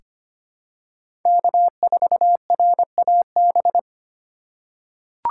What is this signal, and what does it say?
Morse code, heard as a keyed tone, about 25 wpm, 700 Hz. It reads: K4RAB
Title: K1USN SST QSO Practice - 25wpm